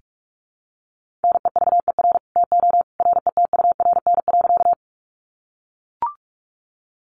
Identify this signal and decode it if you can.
Morse code, heard as a keyed tone, about 44 wpm, 700 Hz. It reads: NEVER TO RETURN.
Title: Sentences from Top 800 Words 45wpm